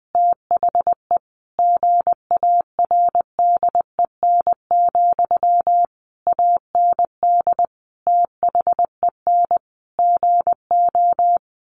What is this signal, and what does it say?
Morse code, heard as a keyed tone, about 20 wpm, 700 Hz. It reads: THE GARDEN, AND THEN GO
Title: Sentences from Top 700 Words 20wpm